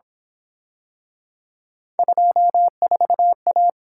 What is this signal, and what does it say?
Morse code, heard as a keyed tone, about 26 wpm, 700 Hz. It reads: 24A
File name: Speed-Racing - Three Letter-Number Combinations - 22wpm